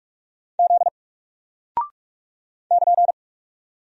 Morse code, heard as a keyed tone, about 44 wpm, 700 Hz. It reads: Z <KN>
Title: Single Character (Letter:Number:Punctuation:Prosign) Alt 45wpm